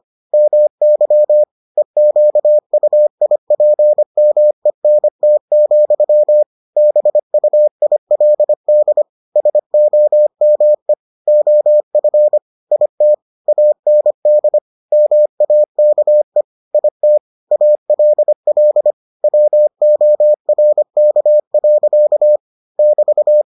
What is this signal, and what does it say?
Morse code, heard as a keyed tone, about 25 wpm, 600 Hz. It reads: MY EQUIPMENT, BUILD SOME OF IT AND MAKE IT ALL WORK. <BT>